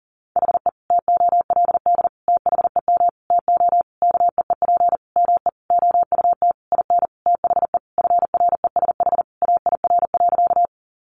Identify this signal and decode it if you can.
Morse code, heard as a keyed tone, about 40 wpm, 700 Hz. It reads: HE TOLD THEM TO KEEP ME OUT IN THE FRESH AIR.
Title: Sentences from Top 1200 Words 40wpm